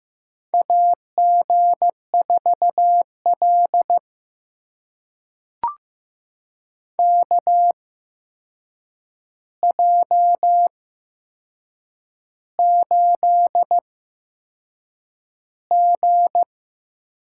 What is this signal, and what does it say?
Morse code, heard as a keyed tone, about 15 wpm, spaced as for 6 wpm, 700 Hz. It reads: AG4L KJ8G
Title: USA Call Signs - 4 Characters - 8x Character Spacing - 15wpm